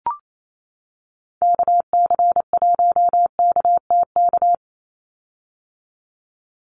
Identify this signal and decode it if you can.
Morse code, heard as a keyed tone, about 28 wpm, 700 Hz. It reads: KC1KTK
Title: USA Call Signs 28wpm